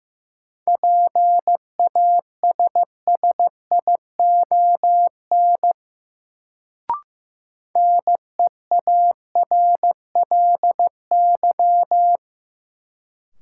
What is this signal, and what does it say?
Morse code, heard as a keyed tone, about 15 wpm, 700 Hz. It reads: PASSION NEARLY
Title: Sentences from Top 1300 Words - New Word Drill - 15wpm